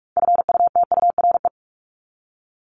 Morse code, heard as a keyed tone, about 44 wpm, 700 Hz. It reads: FUTURE
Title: Top 501-750 Words 45wpm